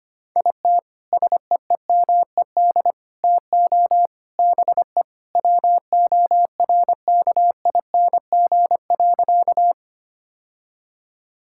Morse code, heard as a keyed tone, about 25 wpm, 700 Hz. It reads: IT SEEMED TO BE WORKING.
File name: practice / 25wpm / Sentences with 3-Word Phrases - 25wpm